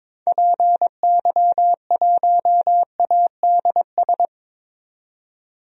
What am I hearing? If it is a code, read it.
Morse code, heard as a keyed tone, about 21 wpm, 700 Hz. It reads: PY1ADS
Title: DXCC-100 Call Signs - 6 Characters - 22wpm